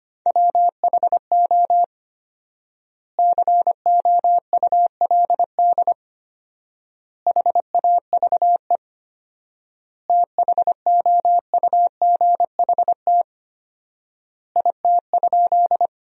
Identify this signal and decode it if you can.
Morse code, heard as a keyed tone, about 25 wpm, 700 Hz. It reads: WHO COULD HAVE THOUGHT IT?
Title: 4x Word-Spacing - Sentences from Top 300 Words - 25wpm